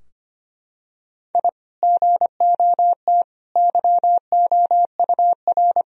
Morse code, heard as a keyed tone, about 25 wpm, 700 Hz. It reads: I GOT YOUR
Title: Sentences from Top 1000 Words - Review Part 2 - 25wpm